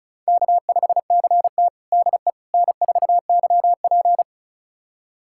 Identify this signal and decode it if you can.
Morse code, heard as a keyed tone, about 36 wpm, 700 Hz. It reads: K5CT DE N4YP
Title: USA Callsign DE Callsign - 4 Characters - 35wpm